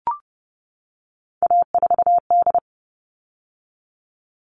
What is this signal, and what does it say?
Morse code, heard as a keyed tone, about 30 wpm, 700 Hz. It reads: A4D
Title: All Cut Numbers - 3-Digit 30wpm